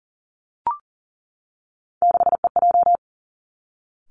Morse code, heard as a keyed tone, about 40 wpm, 700 Hz. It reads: 6EJ
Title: Three Challenging Letter-Number Combinations 40wpm